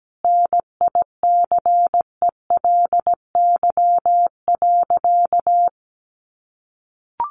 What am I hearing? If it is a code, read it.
Morse code, heard as a keyed tone, about 17 wpm, 700 Hz. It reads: NICELY.
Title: Easy Phrases 17wpm